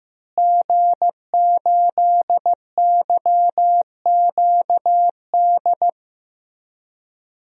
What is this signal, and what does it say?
Morse code, heard as a keyed tone, about 15 wpm, 700 Hz. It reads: G8YQD
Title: DXCC-100 Call Signs - 5 Characters - Repeated 2x - 15wpm